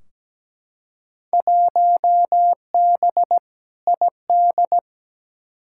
Morse code, heard as a keyed tone, about 17 wpm, 700 Hz. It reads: 1B ID
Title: Field Day Exchanges - 17wpm